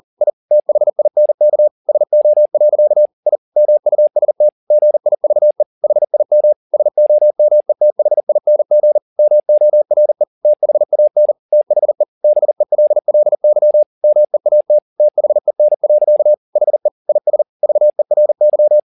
Morse code, heard as a keyed tone, about 40 wpm, 600 Hz. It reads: I THINK SO. I MUST GIVE HIM SOMETHING MORE THAN THE BELLY MEAT THEN. HE IS VERY